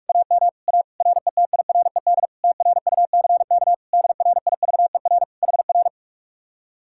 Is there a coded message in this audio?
Yes: Morse code from 0.1 to 5.9 s, heard as a keyed tone, about 69 wpm, spaced as for 56 wpm, 700 Hz. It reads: AM A RETIRED TRUCK DRIVER HR